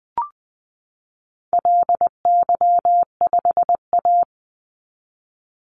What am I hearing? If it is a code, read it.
Morse code, heard as a keyed tone, about 20 wpm, 700 Hz. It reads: LY5A
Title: Call Signs Repeated 2x - 20wpm